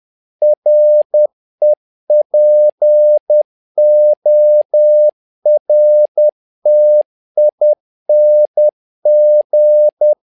Morse code, heard as a keyed tone, about 10 wpm, 600 Hz. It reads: REPORTING
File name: learning center / tower trouble / tower trouble 10wpm